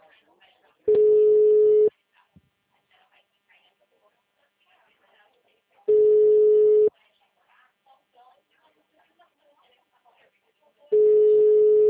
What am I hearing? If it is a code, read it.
Ringback tone (Europe, 425 Hz), with speech.